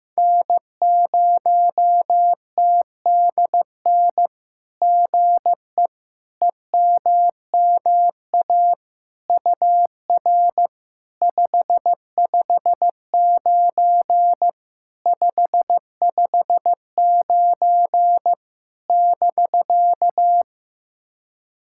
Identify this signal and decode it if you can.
Morse code, heard as a keyed tone, about 15 wpm, 700 Hz. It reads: N0TDN GE EMMA UR 559 559 <BK>
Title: SOTA QSO Practice 15wpm